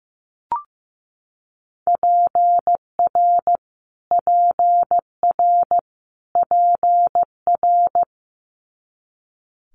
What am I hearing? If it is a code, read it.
Morse code, heard as a keyed tone, about 15 wpm, 700 Hz. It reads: PR PR PR
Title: Multi-letter Phonemes Repeated 3x 15wpm